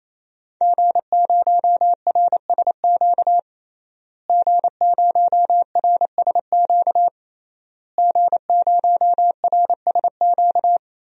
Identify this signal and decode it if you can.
Morse code, heard as a keyed tone, about 28 wpm, 700 Hz. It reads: G0RSQ G0RSQ G0RSQ
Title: DXCC-100 Call Signs - 5 Characters - Repeated 3x - 28wpm